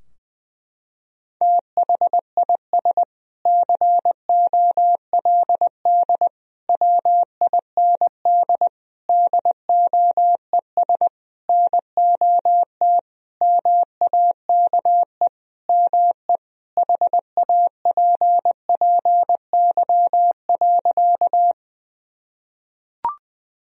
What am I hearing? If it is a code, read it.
Morse code, heard as a keyed tone, about 20 wpm, 700 Hz. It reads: THIS COLD WIND DOES NOT MAKE ME HAPPY.
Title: Sentences from Top 1000 Words - Review Part 2 - 20wpm